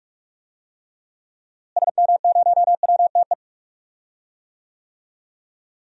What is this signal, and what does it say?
Morse code, heard as a keyed tone, about 46 wpm, 700 Hz. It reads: SM0WTE
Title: DXCC-100 Call Signs - 6 Characters - Repeated 2x - 45wpm